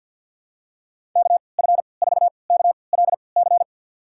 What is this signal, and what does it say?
Morse code, heard as a keyed tone, about 53 wpm, spaced as for 29 wpm, 700 Hz. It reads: KF4KLC
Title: USA Call Signs - 6 Characters - 3x Character Spacing - 50wpm